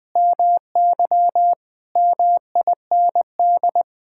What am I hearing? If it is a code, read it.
Morse code, heard as a keyed tone, about 20 wpm, 700 Hz. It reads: MY MIND
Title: Sentences with 6-Word Phrases - 20wpm